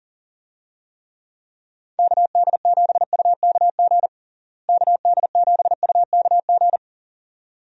Morse code, heard as a keyed tone, about 40 wpm, 700 Hz. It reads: KD7UKG KD7UKG
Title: USA Call Signs - 6 Characters - Repeated 2x - 40wpm